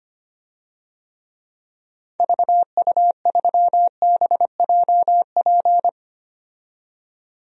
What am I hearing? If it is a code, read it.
Morse code, heard as a keyed tone, about 25 wpm, 700 Hz. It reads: VU3BJP
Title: DXCC-100 Call Signs - 6 Characters - 25wpm